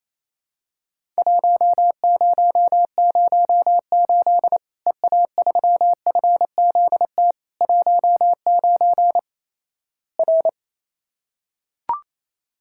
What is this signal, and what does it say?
Morse code, heard as a keyed tone, about 28 wpm, 700 Hz. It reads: 1008 EA3FZT 19 R
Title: Worked All Europe DX Contest - QTC Exchange - 28wpm